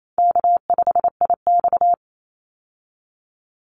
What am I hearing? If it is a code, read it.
Morse code, heard as a keyed tone, about 28 wpm, 700 Hz. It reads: K5IX
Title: Easy Call Signs - 28wpm